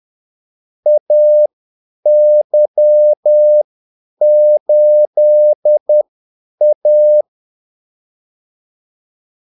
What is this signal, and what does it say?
Morse code, heard as a keyed tone, about 10 wpm, spaced as for 8 wpm, 600 Hz. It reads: AY8A
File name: Callsigns-10-8wpm